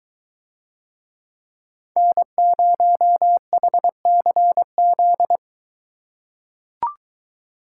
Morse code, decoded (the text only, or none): N0HCZ